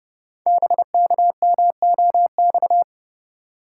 Morse code, heard as a keyed tone, about 30 wpm, 700 Hz. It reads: BKMOX
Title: Code Groups - 5 Letters - 30wpm